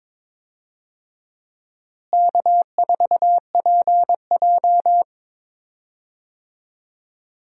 Morse code, heard as a keyed tone, about 22 wpm, 700 Hz. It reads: K4PJ
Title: USA Call Signs - 4 Characters - Repeated 3x - 22wpm